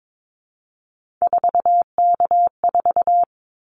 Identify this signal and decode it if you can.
Morse code, heard as a keyed tone, about 22 wpm, 700 Hz. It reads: 4K4
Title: Callsign Prefixes for Top 100 DXCC - 22wpm